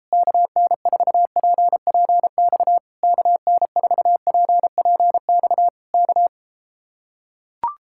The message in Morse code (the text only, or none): KN4PPX KN4PPX K